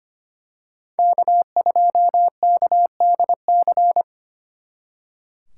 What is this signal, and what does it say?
Morse code, heard as a keyed tone, about 29 wpm, 700 Hz. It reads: K2KDC
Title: Speed Racing - USA 1x3 Call Signs 25wpm